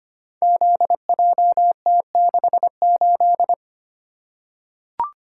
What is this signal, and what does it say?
Morse code, heard as a keyed tone, about 25 wpm, 700 Hz. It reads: ZJT68
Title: Code Groups - 5 Letter:Numbers - 25wpm